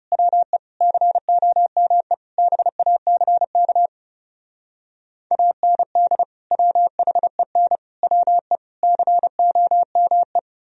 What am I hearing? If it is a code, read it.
Morse code, heard as a keyed tone, about 40 wpm, 700 Hz. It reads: WE COME BACK AND WHEN WE COME